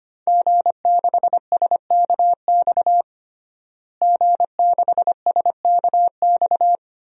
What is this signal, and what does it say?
Morse code, heard as a keyed tone, about 25 wpm, 700 Hz. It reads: G6SKX G6SKX
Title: DXCC-100 Call Signs - 5 Characters - Repeated 2x - 25wpm